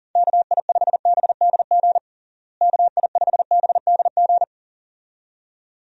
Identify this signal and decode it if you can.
Morse code, heard as a keyed tone, about 40 wpm, 700 Hz. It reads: KI5BDG KI5BDG
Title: USA Call Signs - 6 Characters - Repeated 2x - 40wpm